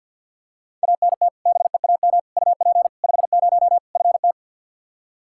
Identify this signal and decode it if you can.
Morse code, heard as a keyed tone, about 47 wpm, 700 Hz. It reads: ANT BEAM UP 50 FT
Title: Practice QSO - Exchange 2 - Info - 50wpm